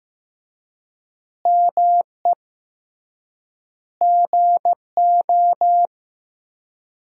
Morse code, heard as a keyed tone, about 16 wpm, 700 Hz. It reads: ME GO
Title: 3x Word-Spacing - Sentences from Top 300 Words - 15wpm